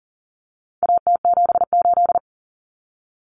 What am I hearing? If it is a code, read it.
Morse code, heard as a keyed tone, about 37 wpm, 700 Hz. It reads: AT78
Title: Common Cut Numbers - 4-Digit 40wpm